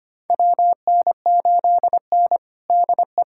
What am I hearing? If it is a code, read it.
Morse code, heard as a keyed tone, about 25 wpm, 700 Hz. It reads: WN8N DE
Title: USA Callsign DE Callsign - 4 Characters - 25wpm